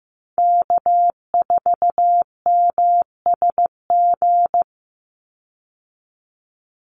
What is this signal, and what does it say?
Morse code, heard as a keyed tone, about 15 wpm, 700 Hz. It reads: K4MSG
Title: Call Signs Repeated 2x - 15wpm